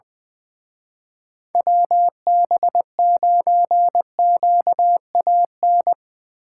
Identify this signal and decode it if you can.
Morse code, heard as a keyed tone, about 22 wpm, 700 Hz. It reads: WB9QAN
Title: Speed Racing - USA 2x3 Call Signs 20wpm